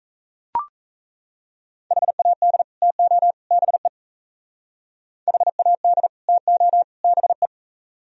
Morse code, heard as a keyed tone, about 37 wpm, 700 Hz. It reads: HAD TO BE HAD TO BE